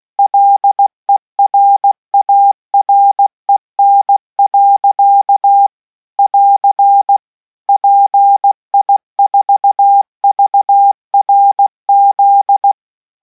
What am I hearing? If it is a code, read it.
Morse code, heard as a keyed tone, about 16 wpm, 800 Hz. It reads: LERAREN. <AR> PI4VRZ